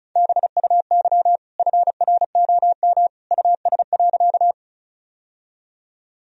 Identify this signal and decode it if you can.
Morse code, heard as a keyed tone, about 36 wpm, 700 Hz. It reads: BUY FROM US.
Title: Sentences from Top 1000 Words - Review Part 4 - 35wpm